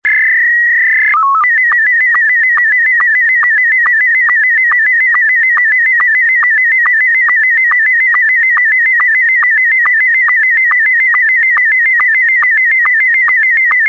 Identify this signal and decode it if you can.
SSTV (slow-scan television), Scottie 1.
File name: sstv / sstv1 / scottie s1